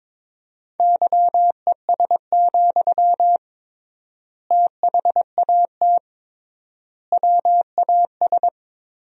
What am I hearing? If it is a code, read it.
Morse code, heard as a keyed tone, about 22 wpm, 700 Hz. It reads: YES, THAT WAS
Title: 3x Word-Spacing - Sentences from Top 500 Words - 22wpm